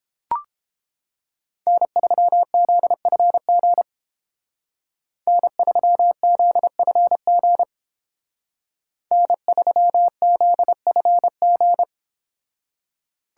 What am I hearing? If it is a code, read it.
Morse code, heard as a keyed tone, about 29 wpm, 700 Hz. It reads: N3ZFG N3ZFG N3ZFG